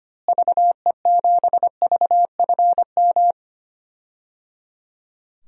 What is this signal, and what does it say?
Morse code, heard as a keyed tone, about 25 wpm, 700 Hz. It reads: VE7VFM
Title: DXCC-100 Call Signs - 6 Characters - Repeated 3x - 25wpm